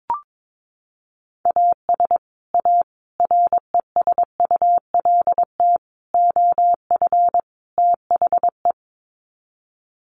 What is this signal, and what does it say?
Morse code, heard as a keyed tone, about 22 wpm, 700 Hz. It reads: AS A RESULT OF THE
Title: Sets of 5 Words 22wpm